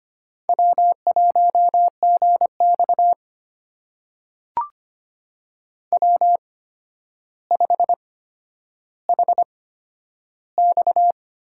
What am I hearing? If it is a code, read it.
Morse code, heard as a keyed tone, about 25 wpm, spaced as for 11 wpm, 700 Hz. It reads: W1GX W5HX
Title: USA Call Signs - 8x Character Spacing - 25wpm